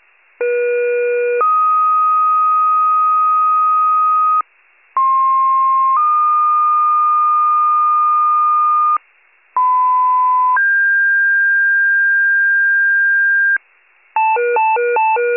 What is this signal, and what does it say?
Two-tone page, heard as a one-second tone, then a three-second tone, about 480 then 1250 Hz, about 1060 then 1220 Hz, about 1010 then 1600 Hz. It is heard from 0.4 to 13.6 s.
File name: pager / 94 RESCUE DUTY / wcalert